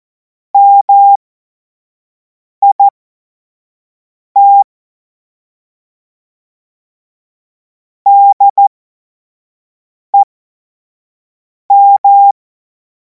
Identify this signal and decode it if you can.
Morse code, heard as a keyed tone, about 14 wpm, spaced as for 5 wpm, 800 Hz. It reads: MIT DEM
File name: Morsekurs / Witze / Witz 03. Februar 2026 25 F70